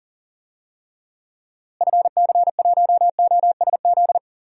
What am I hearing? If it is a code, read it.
Morse code, heard as a keyed tone, about 40 wpm, 700 Hz. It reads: FC1OSZ
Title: DXCC-100 Call Signs - 6 Characters - Repeated 3x - 40wpm